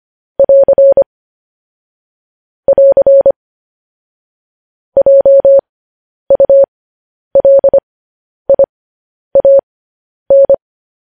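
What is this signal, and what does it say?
Morse code, heard as a keyed tone, about 25 wpm, spaced as for 10 wpm, 550 Hz. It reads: <AR> <AR> JULIAN
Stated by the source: Morse code news headlines.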